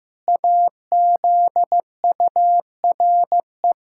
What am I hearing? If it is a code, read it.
Morse code, heard as a keyed tone, about 16 wpm, 700 Hz. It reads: AZURE
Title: Pangram Challenge v2 - 15wpm